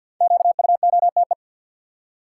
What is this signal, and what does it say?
Morse code, heard as a keyed tone, about 47 wpm, 700 Hz. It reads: QUOTE
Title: Five Letter Words - 50wpm